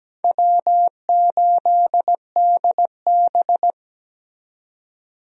Morse code, heard as a keyed tone, about 17 wpm, 700 Hz. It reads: W8DB
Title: USA Call Signs - 4 Characters - 17wpm